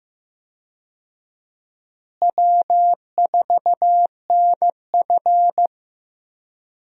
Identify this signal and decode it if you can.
Morse code, heard as a keyed tone, about 16 wpm, 700 Hz. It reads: W4NF
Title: USA Call Signs - 4 Characters - Repeated 2x - 15wpm